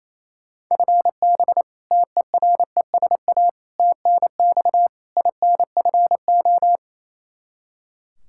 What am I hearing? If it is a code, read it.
Morse code, heard as a keyed tone, about 28 wpm, 700 Hz. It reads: FB TERESA TNX INFO
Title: Practice QSO - Exchange 3 - More Info - 28wpm